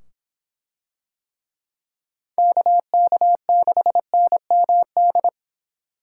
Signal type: Morse code

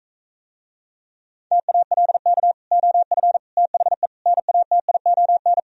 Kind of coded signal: Morse code

